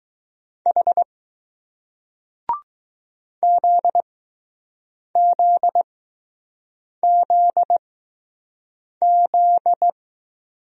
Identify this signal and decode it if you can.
Morse code, heard as a keyed tone, about 20 wpm, spaced as for 6 wpm, 700 Hz. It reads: H ZZZZ